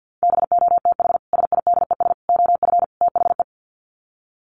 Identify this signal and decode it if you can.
Morse code, heard as a keyed tone, about 48 wpm, 700 Hz. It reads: BOTH SIDES OF THE
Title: Sets of 4 Words 50wpm